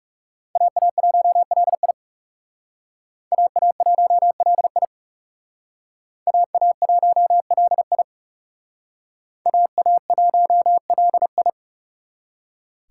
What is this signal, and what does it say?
Morse code, heard as a keyed tone, about 37 wpm, 700 Hz. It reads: AA1LI AA1LI AA1LI AA1LI
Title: Speed Racing - USA 2x2 Call Signs 30wpm